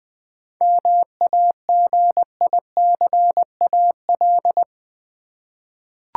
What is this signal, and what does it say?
Morse code, heard as a keyed tone, about 20 wpm, 700 Hz. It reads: MAGICAL
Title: 7 Letter Words - First 3 Letters Given - 20wpm